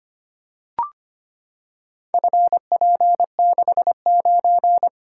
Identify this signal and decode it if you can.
Morse code, heard as a keyed tone, about 25 wpm, 700 Hz. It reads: FP69
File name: practice / 25wpm / Four Letter-Number Combinations - 25wpm